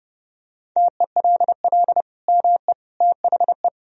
Morse code, heard as a keyed tone, about 32 wpm, 700 Hz. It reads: TELL ME THE
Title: Sentences from Top 1000 Words - Review Part 4 - 30wpm